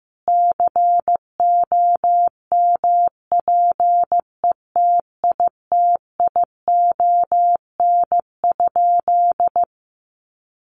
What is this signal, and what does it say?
Morse code, heard as a keyed tone, about 15 wpm, 700 Hz. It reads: COMPETITION?
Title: Sentences from Top 1000 Words 15wpm